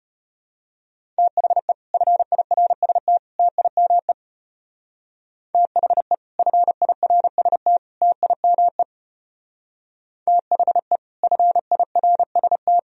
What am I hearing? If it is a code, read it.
Morse code, heard as a keyed tone, about 29 wpm, 700 Hz. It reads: THE FIRST TIME THE FIRST TIME THE FIRST
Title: Speed-Racing - Sets of 3 Words 25wpm